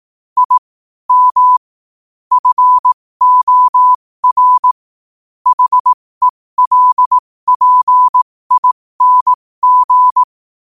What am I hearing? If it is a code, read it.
Morse code, heard as a keyed tone, about 18 wpm, spaced as for 15 wpm, 1000 Hz. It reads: I M FOR HELPING
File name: qotdfiles / 15wpm